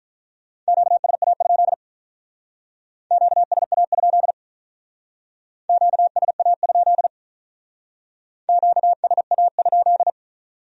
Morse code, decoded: QSA? QSA? QSA? QSA?

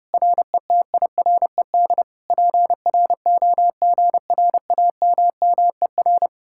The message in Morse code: RETIRED PROGRAMMER

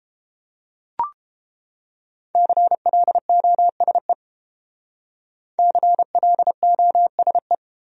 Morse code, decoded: CLOSE CLOSE

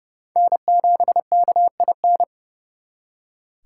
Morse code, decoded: N7KIN

